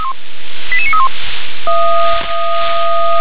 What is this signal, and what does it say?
Five-tone selective call (ZVEI), heard as quick runs of five tones, tones of about 80 ms. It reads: 88931